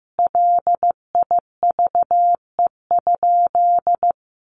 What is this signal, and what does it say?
Morse code, heard as a keyed tone, about 14 wpm, 700 Hz. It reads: LIVE?
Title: Sentences from Top 1500 Words 15wpm